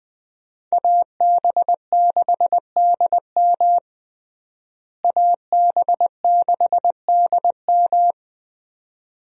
Morse code, heard as a keyed tone, about 21 wpm, 700 Hz. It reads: AB6DM AB6DM
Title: USA Call Signs - 5 Characters - Repeated 2x - 20wpm